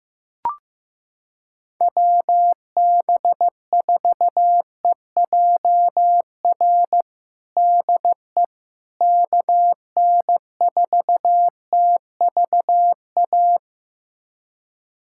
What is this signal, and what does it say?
Morse code, heard as a keyed tone, about 15 wpm, 700 Hz. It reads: WB4EJR DE KN4TVA
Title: USA Callsign DE Callsign 15wpm